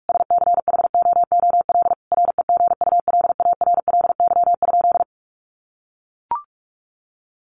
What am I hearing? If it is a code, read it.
Morse code, heard as a keyed tone, about 44 wpm, 700 Hz. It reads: SCHOOL REGULARLY?
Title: Sentences from Top 1400 Words 45wpm